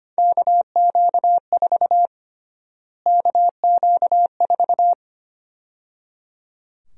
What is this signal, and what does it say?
Morse code, heard as a keyed tone, about 25 wpm, 700 Hz. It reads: KQ4 KQ4
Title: USA Call Sign Prefixes - Repeated 2x - 25wpm